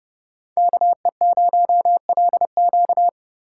Morse code, heard as a keyed tone, about 32 wpm, 700 Hz. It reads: KE0LQ
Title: USA Call Signs - 5 Characters - 4x Character Spacing - 30wpm